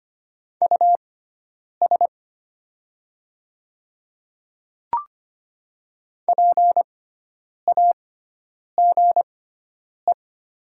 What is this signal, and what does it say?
Morse code, heard as a keyed tone, about 25 wpm, spaced as for 9 wpm, 700 Hz. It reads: US PAGE
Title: 6x Character Spacing - Top 500 Words - 25wpm